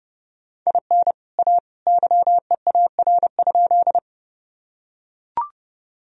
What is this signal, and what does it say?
Morse code, heard as a keyed tone, about 32 wpm, 700 Hz. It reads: IN A YEAR?
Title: Sentences from Top 1000 Words - Review Part 1 - 30wpm